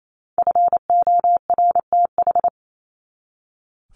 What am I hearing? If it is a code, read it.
Morse code, heard as a keyed tone, about 28 wpm, 700 Hz. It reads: FORTH
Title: Top 1751-2000 Words 28wpm